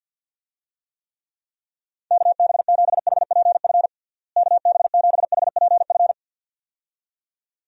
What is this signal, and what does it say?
Morse code, heard as a keyed tone, about 50 wpm, 700 Hz. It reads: KB7HPF KB7HPF
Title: USA Call Signs - 6 Characters - Repeated 2x - 50wpm